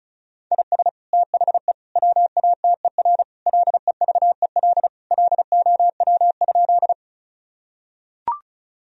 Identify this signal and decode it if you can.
Morse code, heard as a keyed tone, about 36 wpm, 700 Hz. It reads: IS THE WATER LEVEL LOW?
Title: Sentences from Top 1000 Words - Review Part 4 - 35wpm